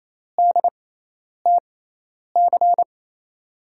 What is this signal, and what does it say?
Morse code, heard as a keyed tone, about 28 wpm, spaced as for 10 wpm, 700 Hz. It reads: DTC
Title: USA Call Signs - 6 Characters - 6x Character Spacing - 28wpm